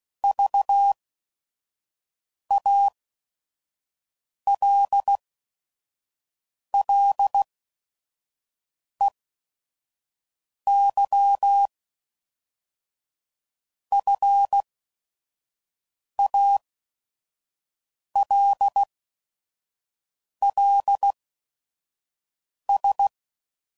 Morse code, heard as a keyed tone, about 16 wpm, spaced as for 5 wpm, 800 Hz. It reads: VALLEYFALLS
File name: CW Code Groups @5WPM / fromSM's / Cities